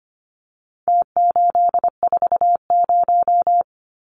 Morse code, heard as a keyed tone, about 25 wpm, 700 Hz. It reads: T840
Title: Numbers - Contest Style - 4-Digit 25wpm